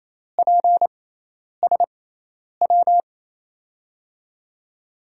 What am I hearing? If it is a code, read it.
Morse code, heard as a keyed tone, about 28 wpm, spaced as for 10 wpm, 700 Hz. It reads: PSW
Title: USA Call Signs - 6 Characters - 6x Character Spacing - 28wpm